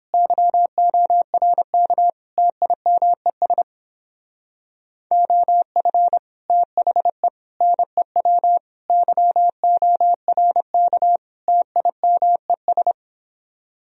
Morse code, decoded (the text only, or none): YORK TIMES OF THE NEW YORK TIMES